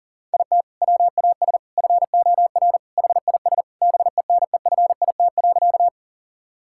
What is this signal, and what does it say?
Morse code, heard as a keyed tone, about 40 wpm, 700 Hz. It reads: IT WAS FOR HIS BENEFIT.
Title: Sentences from Top 1000 Words - Review Part 6 - 40wpm